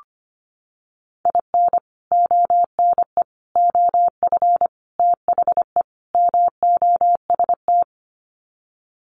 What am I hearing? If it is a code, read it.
Morse code, heard as a keyed tone, about 25 wpm, 690 Hz. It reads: IN ONE OF THE MOST
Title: Sets of 5 Words Encore 25wpm